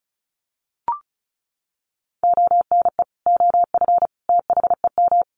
Morse code, heard as a keyed tone, about 34 wpm, 700 Hz. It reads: ONE OF THEM
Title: Sets of 3 Words 35wpm